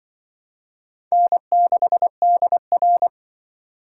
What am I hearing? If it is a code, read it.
Morse code, heard as a keyed tone, about 25 wpm, 700 Hz. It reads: N6DR
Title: Speed Racing - USA 1x2 Call Signs 20wpm